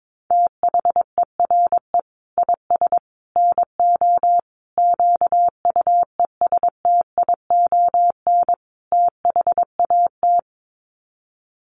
Morse code, heard as a keyed tone, about 22 wpm, 700 Hz. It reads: THERE IS NO QUESTION THAT
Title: Sets of 5 Words Encore 22wpm